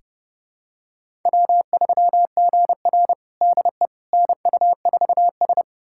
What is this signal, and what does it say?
Morse code, heard as a keyed tone, about 30 wpm, 700 Hz. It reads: W3GR DE NU4S
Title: USA Callsign DE Callsign - 4 Characters - 30wpm